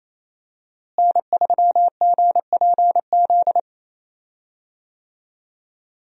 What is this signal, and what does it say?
Morse code, heard as a keyed tone, about 28 wpm, 700 Hz. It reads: N3GPZ
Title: USA Call Signs - 5 Characters - Repeated 2x - 28wpm